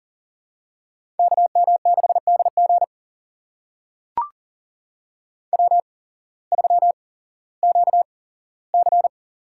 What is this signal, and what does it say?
Morse code, heard as a keyed tone, about 40 wpm, spaced as for 19 wpm, 700 Hz. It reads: KM6DG W3QC
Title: USA Call Signs - 8x Character Spacing - 40wpm